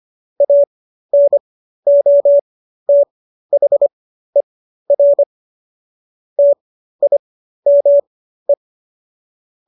Morse code, heard as a keyed tone, about 25 wpm, spaced as for 13 wpm, 570 Hz. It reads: ANOTHER TIME